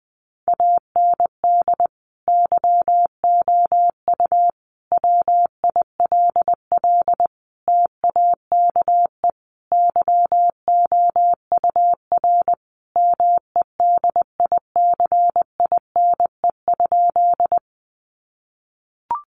Morse code, heard as a keyed tone, about 20 wpm, 700 Hz. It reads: AND YOU WILL TAKE YOUR MEDICINE?
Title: Sentences from Top 900 Words 20wpm